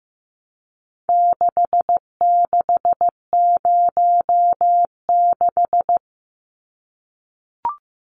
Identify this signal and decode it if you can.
Morse code, heard as a keyed tone, about 15 wpm, 700 Hz. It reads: 6606
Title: Numbers - 4-Digit 15wpm